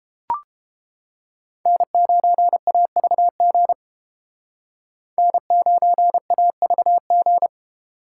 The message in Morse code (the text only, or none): N9AVG N9AVG